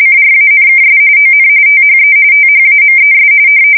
RTTY, about 45.45 baud. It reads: CRANKS ON THE THAMES
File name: fsk170 / printing-telegraph